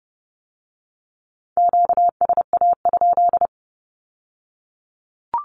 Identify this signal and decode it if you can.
Morse code, heard as a keyed tone, about 30 wpm, 700 Hz. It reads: QSA?
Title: Q-Codes with Spoken Meaning - 30wpm